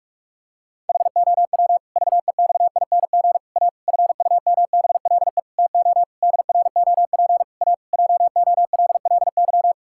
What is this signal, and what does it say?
Morse code, heard as a keyed tone, about 46 wpm, 700 Hz. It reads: HOW VEXING A FUMBLE TO DROP A JOLLY
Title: Pangram Challenge v2 - 45wpm